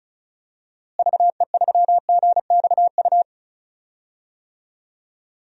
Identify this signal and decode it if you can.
Morse code, heard as a keyed tone, about 36 wpm, 700 Hz. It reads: VE3GXU
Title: DXCC-100 Call Signs - 6 Characters - Repeated 2x - 35wpm